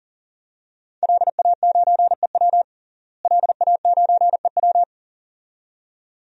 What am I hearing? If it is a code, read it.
Morse code, heard as a keyed tone, about 40 wpm, 700 Hz. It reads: LA9EW LA9EW
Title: DXCC-100 Call Signs - 5 Characters - Repeated 2x - 40wpm